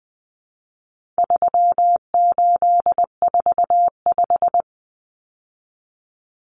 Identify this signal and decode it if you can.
Morse code, heard as a keyed tone, about 20 wpm, 700 Hz. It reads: 3845
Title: Common Cut Numbers - 4-Digit 20wpm